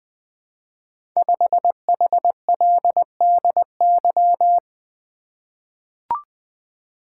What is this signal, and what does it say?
Morse code, heard as a keyed tone, about 20 wpm, 700 Hz. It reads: 5HLDY
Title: Code Groups - 5 Letter:Numbers - 20wpm